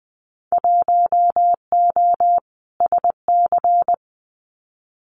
Morse code, heard as a keyed tone, about 20 wpm, 700 Hz. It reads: 1O SC
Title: Winter Field Day Exchanges 20wpm